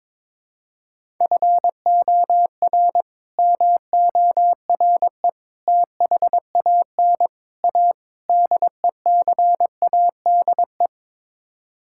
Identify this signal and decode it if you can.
Morse code, heard as a keyed tone, about 22 wpm, 700 Hz. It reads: FOR MORE THAN A DECADE
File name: practice / 22wpm / Build A Phrase - 5-Word Phrases - 22wpm